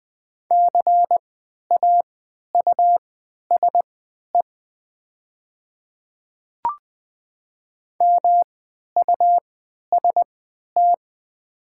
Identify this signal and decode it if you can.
Morse code, heard as a keyed tone, about 20 wpm, spaced as for 11 wpm, 700 Hz. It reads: CAUSE MUST
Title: 3x Character Spacing - Top 200 Words - 20wpm